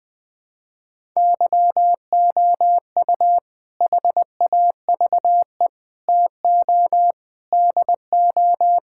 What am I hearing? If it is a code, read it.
Morse code, heard as a keyed tone, about 20 wpm, 700 Hz. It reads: YOU HAVE TO DO
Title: Sentences with 4-Word Phrases - Spoken Beginning - 20wpm